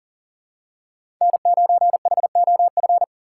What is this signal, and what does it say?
Morse code, heard as a keyed tone, about 38 wpm, 700 Hz. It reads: N9HOF
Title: USA Call Signs - 5 Characters - 4x Character Spacing - 40wpm